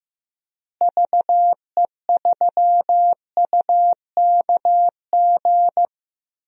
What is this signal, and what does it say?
Morse code, heard as a keyed tone, about 15 wpm, 700 Hz. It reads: VE3UKG
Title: DXCC-100 Call Signs - 6 Characters - Repeated 3x - 15wpm